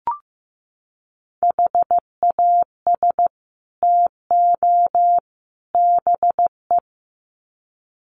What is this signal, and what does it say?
Morse code, heard as a keyed tone, about 15 wpm, 700 Hz. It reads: HAS TO BE
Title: Sets of 3 Words 15wpm